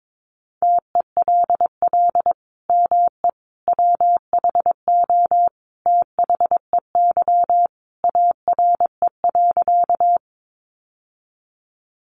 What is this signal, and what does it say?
Morse code, heard as a keyed tone, about 22 wpm, 700 Hz. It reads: TELL ME WHO THEY ARE.
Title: Sentences from Top 700 Words 22wpm